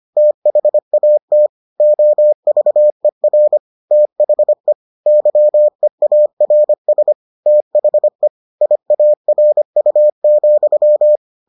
Morse code, heard as a keyed tone, about 25 wpm, 600 Hz. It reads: THAT OVER THE YEARS THE IARU,